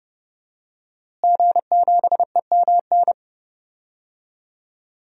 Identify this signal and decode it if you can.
Morse code, heard as a keyed tone, about 30 wpm, 700 Hz. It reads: G7EMN
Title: DXCC-100 Call Signs - 5 Characters - Repeated 3x - 30wpm